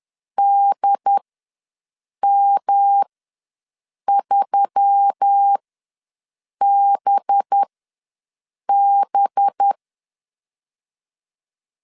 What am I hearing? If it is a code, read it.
Morse code, heard as a keyed tone, about 10 wpm, spaced as for 6 wpm, 800 Hz. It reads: DM3BB
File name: LEZIONI / 5 LBVC / 5 LB ws 10wpm 3616